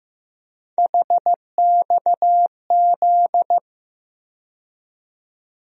Morse code, heard as a keyed tone, about 15 wpm, 700 Hz. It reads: HXZ
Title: USA Call Sign Postfixes - 3 Letters 15wpm